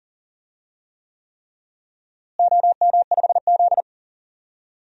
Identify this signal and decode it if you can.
Morse code, heard as a keyed tone, about 37 wpm, 700 Hz. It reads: OM5Z
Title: DXCC-100 Call Signs - 4 Characters - Repeated 3x - 40wpm